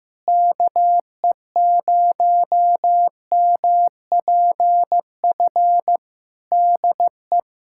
Morse code, heard as a keyed tone, about 15 wpm, 700 Hz. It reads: KE0MPF DE
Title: Practice QSO - Exchange 4 - Ending - 15wpm